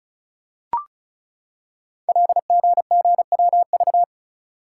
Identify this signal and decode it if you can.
Morse code, heard as a keyed tone, about 36 wpm, 700 Hz. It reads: LGGWV
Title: Code Groups - 5 Letters - 35wpm